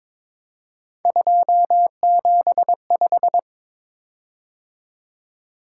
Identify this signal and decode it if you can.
Morse code, heard as a keyed tone, about 22 wpm, 700 Hz. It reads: 275
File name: practice / 22wpm / Three Letter-Number Combinations - 22wpm